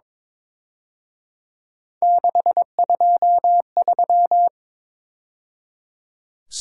Morse code, heard as a keyed tone, about 26 wpm, 700 Hz. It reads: 623
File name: Speed-Racing - Three Letter-Number Combinations - 22wpm